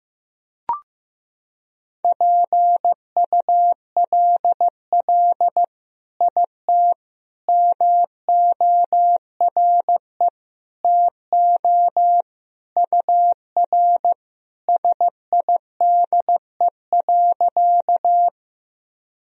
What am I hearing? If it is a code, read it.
Morse code, heard as a keyed tone, about 15 wpm, 700 Hz. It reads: PULL IT MORE TO UR SIDE.
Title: Sentences from Top 400 Words and Abbreviations 15wpm